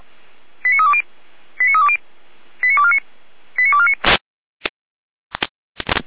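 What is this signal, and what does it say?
Five-tone selective call (ZVEI), heard as quick runs of five tones, tones of about 70 ms. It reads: 89320 89320 89249 89249
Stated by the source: ZVEI five-tone selective call 89337, received on SDR.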